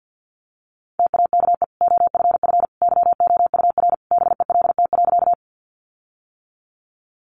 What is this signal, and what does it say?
Morse code, heard as a keyed tone, about 50 wpm, 700 Hz. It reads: TAKE OFF YOUR BELT.
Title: Sentences from Top 1400 Words 50wpm